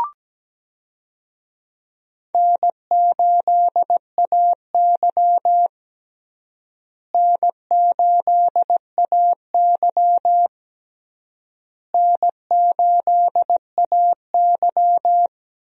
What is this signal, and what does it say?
Morse code, heard as a keyed tone, about 17 wpm, 700 Hz. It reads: N8AY N8AY N8AY
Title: USA Call Signs - 4 Characters - Repeated 3x - 17wpm